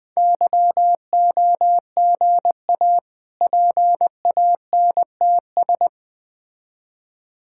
Morse code, heard as a keyed tone, about 20 wpm, 700 Hz. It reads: YOGA PANTS
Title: Difficult Words with Hints - 20wpm